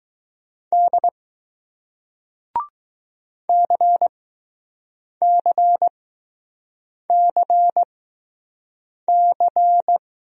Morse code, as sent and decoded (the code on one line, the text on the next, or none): -.. / -.-. -.-. -.-. -.-.
D CCCC